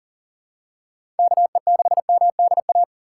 Morse code, heard as a keyed tone, about 40 wpm, 700 Hz. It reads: KE6MDA
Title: USA Call Signs - 3x Character Spacing - 40wpm